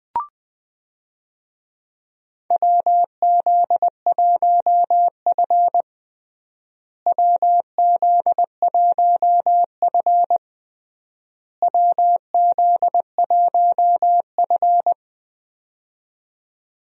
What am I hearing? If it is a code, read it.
Morse code, heard as a keyed tone, about 20 wpm, 700 Hz. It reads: WZ1F WZ1F WZ1F
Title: USA Call Signs - 4 Characters - Repeated 3x - 20wpm